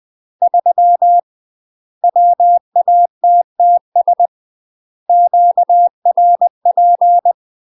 Morse code, heard as a keyed tone, about 20 wpm, 700 Hz. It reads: 3 WATTS QRP
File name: UK MASTER QSOs TRAINING / 20WPM / UK MASTER QSOs 0038